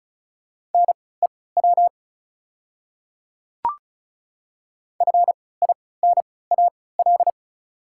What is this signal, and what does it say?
Morse code, heard as a keyed tone, about 36 wpm, spaced as for 20 wpm, 690 Hz. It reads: NEW FINAL
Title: 3x Character Spacing - Top 500 Words - 35wpm